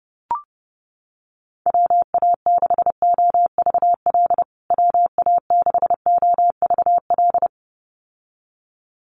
Morse code, heard as a keyed tone, about 30 wpm, 700 Hz. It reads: WA6OVL WA6OVL
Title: USA Call Signs Repeated 2x 30wpm